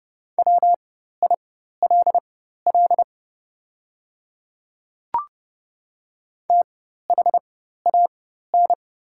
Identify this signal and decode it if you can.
Morse code, heard as a keyed tone, about 30 wpm, spaced as for 14 wpm, 700 Hz. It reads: WILL THAN
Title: 4x Character Spacing - Top 100 Words - 30wpm